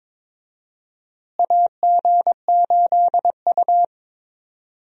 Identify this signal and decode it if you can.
Morse code, heard as a keyed tone, about 22 wpm, 700 Hz. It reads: AG8U
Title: USA Call Signs - 4 Characters - Repeated 3x - 22wpm